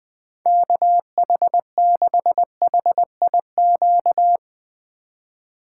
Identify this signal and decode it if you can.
Morse code, heard as a keyed tone, about 21 wpm, 700 Hz. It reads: KH6HIQ
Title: USA Call Signs - 6 Characters - 3x Character Spacing - 20wpm